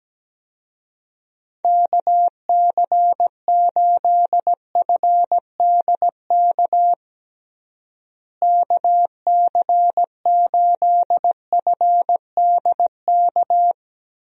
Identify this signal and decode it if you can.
Morse code, heard as a keyed tone, about 17 wpm, 700 Hz. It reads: KC8FDK KC8FDK
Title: USA Call Signs - 6 Characters - Repeated 3x - 17wpm